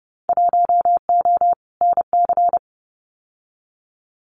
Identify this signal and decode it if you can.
Morse code, heard as a keyed tone, about 30 wpm, 700 Hz. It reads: 1O NC
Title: Winter Field Day Exchanges Encore 30wpm